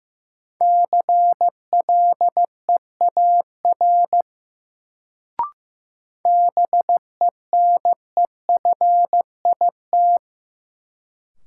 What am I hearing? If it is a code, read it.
Morse code, heard as a keyed tone, about 15 wpm, 700 Hz. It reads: CLEAR BENEFIT
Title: Sets of 5 Words Encore - New Word Drill - 15wpm